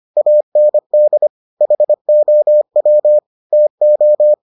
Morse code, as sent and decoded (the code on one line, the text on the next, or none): .- -. -.. / .... --- .-- / - ---
AND HOW TO